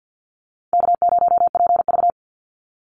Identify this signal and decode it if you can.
Morse code, heard as a keyed tone, about 50 wpm, 700 Hz. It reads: K0PV
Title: Easy Call Signs Repeated 2x - 50wpm